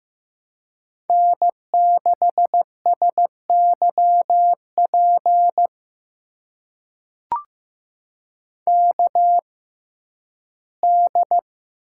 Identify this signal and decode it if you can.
Morse code, heard as a keyed tone, about 15 wpm, spaced as for 11 wpm, 700 Hz. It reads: N6SYP KD
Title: USA Call Signs - 5 Characters - 6x Character Spacing - 15wpm